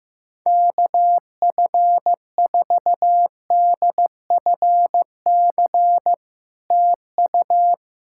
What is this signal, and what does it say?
Morse code, heard as a keyed tone, about 15 wpm, 700 Hz. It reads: KF4DFC TU
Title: SOTA QSO Practice 15wpm